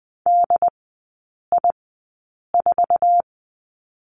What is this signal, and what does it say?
Morse code, heard as a keyed tone, about 20 wpm, spaced as for 8 wpm, 700 Hz. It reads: DI4
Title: 087 Lesson 15e - Number 5 - Three Characters 20wpm